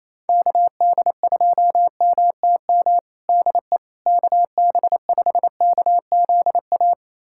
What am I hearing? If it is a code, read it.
Morse code, heard as a keyed tone, about 28 wpm, 700 Hz. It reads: KD2MTM DE KB5KZA
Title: USA Callsign DE Callsign - 6 Characters - 28wpm